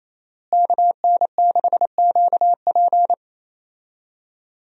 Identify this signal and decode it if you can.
Morse code, heard as a keyed tone, about 27 wpm, 700 Hz. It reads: KN6QP
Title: USA Call Signs - 5 Characters - Repeated 3x - 28wpm